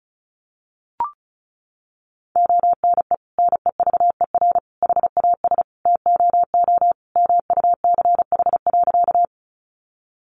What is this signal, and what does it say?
Morse code, heard as a keyed tone, about 34 wpm, 700 Hz. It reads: ONE NEVER HAS TOO MUCH.
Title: Sentences from Top 300 Words 35wpm